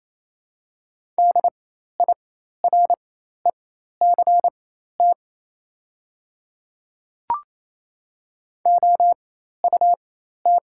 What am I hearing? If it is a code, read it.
Morse code, heard as a keyed tone, about 28 wpm, spaced as for 13 wpm, 690 Hz. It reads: DIRECT OUT
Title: 4x Character Spacing - Top 500 Words - 28wpm